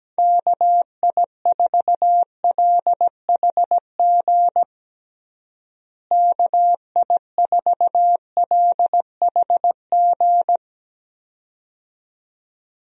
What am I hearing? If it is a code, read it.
Morse code, heard as a keyed tone, about 17 wpm, 700 Hz. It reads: KI4LHG KI4LHG
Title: USA Call Signs - 6 Characters - Repeated 2x - 17wpm